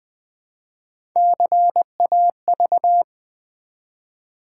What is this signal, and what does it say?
Morse code, heard as a keyed tone, about 20 wpm, 700 Hz. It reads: CAV
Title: Build A Word - 1 Letter At A Time - 4-Letter Words - 20wpm